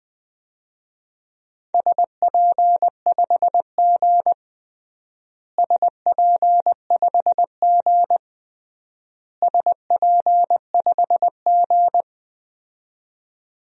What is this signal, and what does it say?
Morse code, heard as a keyed tone, about 20 wpm, 700 Hz. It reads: SP5G SP5G SP5G
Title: DXCC-100 Call Signs - 4 Characters - Repeated 3x - 20wpm